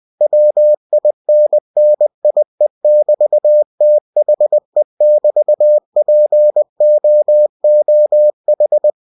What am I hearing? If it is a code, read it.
Morse code, heard as a keyed tone, about 20 wpm, 600 Hz. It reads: WINNIE<BT>THE<BT>POOH
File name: cw-books / winnie-the-pooh / winnie-20wpm-page-0022